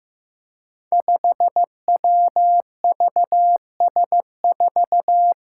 Morse code, heard as a keyed tone, about 15 wpm, 700 Hz. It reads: 5WVS4
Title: Five Challenging Letter-Number Combinations 15wpm